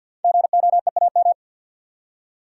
Morse code, heard as a keyed tone, about 52 wpm, 700 Hz. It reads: GOEAM
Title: Code Groups - 5 Letters - 50wpm